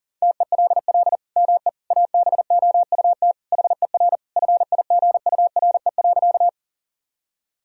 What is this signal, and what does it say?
Morse code, heard as a keyed tone, about 37 wpm, 700 Hz. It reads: TELL ME ABOUT HER FIGURE.
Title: Sentences from Top 1000 Words - Review Part 2 - 40wpm